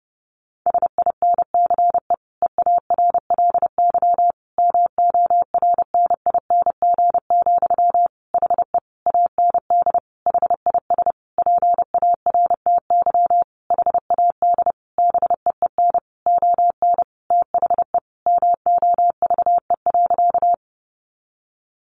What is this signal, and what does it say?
Morse code, heard as a keyed tone, about 30 wpm, 700 Hz. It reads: SINCE EARLY MORNING, HE AND HIS PARTY HAD BEEN ON THE MOVE.
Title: Sentences from Top 1400 Words 30wpm